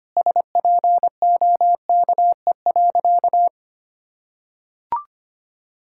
Morse code, heard as a keyed tone, about 25 wpm, 700 Hz. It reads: SPOKE.
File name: practice / 25wpm / Sentences with 3-Word Phrases - 25wpm